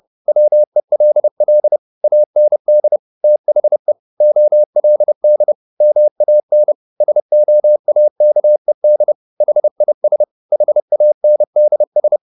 Morse code, heard as a keyed tone, about 30 wpm, 600 Hz. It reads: WELL AND THE OLD MAN SOAKED HIS HANDS